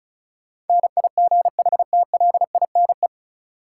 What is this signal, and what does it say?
Morse code, heard as a keyed tone, about 36 wpm, 700 Hz. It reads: NIGHTLINE
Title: Sets of 5 Words - New Word Drill - 35wpm